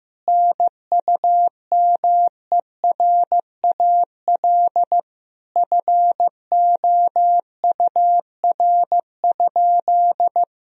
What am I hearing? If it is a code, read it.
Morse code, heard as a keyed tone, about 15 wpm, 700 Hz. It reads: NUMERAL FOUR?
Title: Sentences from Top 1000 Words - Review Part 2 - 15wpm